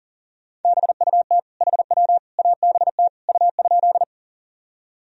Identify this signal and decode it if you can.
Morse code, heard as a keyed tone, about 40 wpm, 700 Hz. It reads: BUT HW ABT U?
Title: Sentences from Top 100 Words and Abbreviations 40wpm